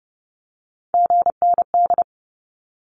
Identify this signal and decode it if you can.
Morse code, heard as a keyed tone, about 30 wpm, 700 Hz. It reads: GND
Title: Abbreviations 30wpm